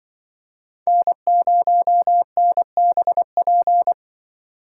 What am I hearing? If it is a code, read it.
Morse code, heard as a keyed tone, about 24 wpm, 700 Hz. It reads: N0NBP
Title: Speed Racing - USA 1x3 Call Signs 20wpm